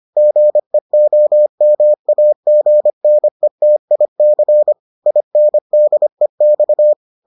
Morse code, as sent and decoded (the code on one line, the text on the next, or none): --. --- -- .- --. -. - .. -.-. / .. -. -.. -..-
GEOMAGNETIC INDEX